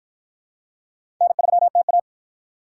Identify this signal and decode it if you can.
Morse code, heard as a keyed tone, about 53 wpm, 700 Hz. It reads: N3TA